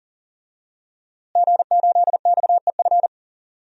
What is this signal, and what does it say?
Morse code, heard as a keyed tone, about 40 wpm, 700 Hz. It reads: G8XEF
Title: DXCC-100 Call Signs - 5 Characters - Repeated 2x - 40wpm